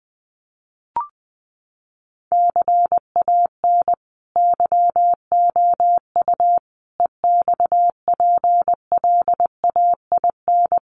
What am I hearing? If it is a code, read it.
Morse code, heard as a keyed tone, about 19 wpm, 700 Hz. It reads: CAN YOU EXPLAIN
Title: Sentences from Top 1100 Words 20wpm